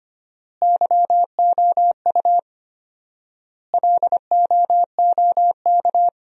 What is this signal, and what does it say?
Morse code, heard as a keyed tone, about 25 wpm, 700 Hz. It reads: YOU LOOK
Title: 4x Word-Spacing - Sentences from Top 100 Words - 25wpm